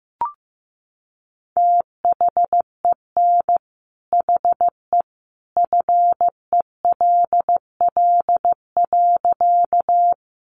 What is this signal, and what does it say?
Morse code, heard as a keyed tone, about 15 wpm, 700 Hz. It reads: THEN HE FELL.
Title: A Sound Of Thunder 15wpm